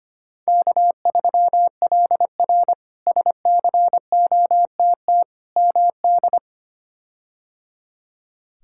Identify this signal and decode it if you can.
Morse code, heard as a keyed tone, about 25 wpm, 700 Hz. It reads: K3LR SCOTT MD
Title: K1USN SST QSO Practice - 25wpm